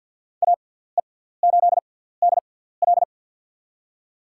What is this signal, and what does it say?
Morse code, heard as a keyed tone, about 53 wpm, spaced as for 17 wpm, 700 Hz. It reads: AE8DL
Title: USA Call Signs - 5 Characters - 6x Character Spacing - 50wpm